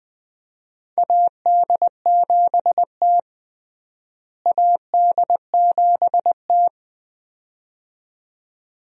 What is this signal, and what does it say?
Morse code, heard as a keyed tone, about 20 wpm, 700 Hz. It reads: AD7T AD7T
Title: USA Call Signs - 4 Characters - Repeated 2x - 20wpm